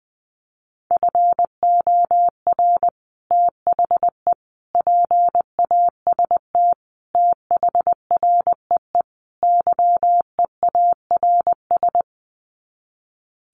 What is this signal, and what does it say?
Morse code, heard as a keyed tone, about 20 wpm, 700 Hz. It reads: FOR THE PAST THREE YEARS
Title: Sets of 5 Words Encore2 20wpm